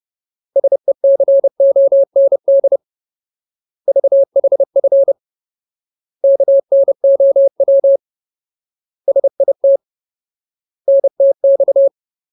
Morse code, heard as a keyed tone, about 30 wpm, 550 Hz. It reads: SECOND VHF KNOW SIT NTX
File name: Ultimate Word List / 30WPM Ultimate